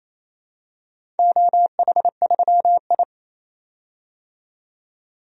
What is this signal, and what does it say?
Morse code, heard as a keyed tone, about 28 wpm, 700 Hz. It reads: OH3I
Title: DXCC-100 Call Signs - 4 Characters - 28wpm